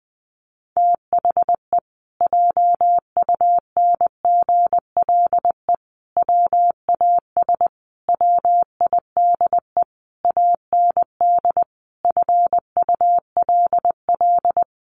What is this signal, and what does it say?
Morse code, heard as a keyed tone, about 20 wpm, 700 Hz. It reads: THE JUNGLE WAS WIDE AND FULL
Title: A Sound Of Thunder 20wpm